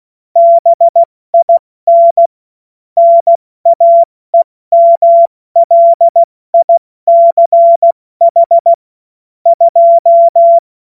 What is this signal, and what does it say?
Morse code, heard as a keyed tone, about 16 wpm, 680 Hz. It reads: BIN NAEMLICH 2
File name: Morsekurs / Witze / Witz 03. August 2021, 70 F80